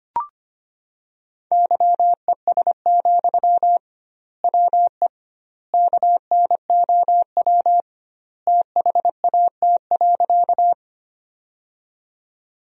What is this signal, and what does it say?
Morse code, heard as a keyed tone, about 25 wpm, 700 Hz. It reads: YES, WE KNOW THAT.
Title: 2x Word-Spacing - Sentences from Top 500 Words - 25wpm